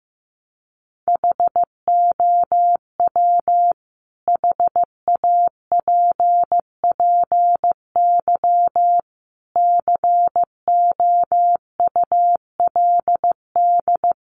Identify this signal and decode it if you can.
Morse code, heard as a keyed tone, about 15 wpm, 700 Hz. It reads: HOW HAPPY COULD
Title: Sentences from Top 1000 Words 15wpm